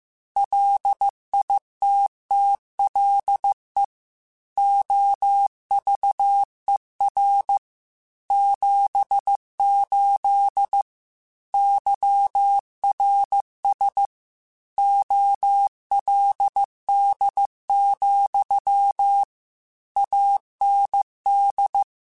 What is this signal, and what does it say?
Morse code, heard as a keyed tone, about 15 wpm, 800 Hz. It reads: LITTLE OVER 78 YRS OLD, AND
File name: QSOs / QSOs-mp3s / kc7sac@13